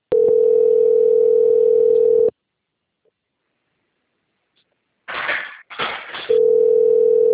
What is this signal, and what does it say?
Ringback tone (North America), 440 + 480 Hz.